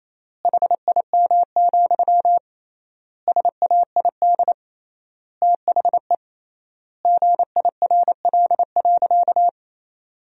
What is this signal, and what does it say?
Morse code, heard as a keyed tone, about 28 wpm, 700 Hz. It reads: HIM, SAID THE GIRL.
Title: 3x Word-Spacing - Sentences from Top 400 Words - 28wpm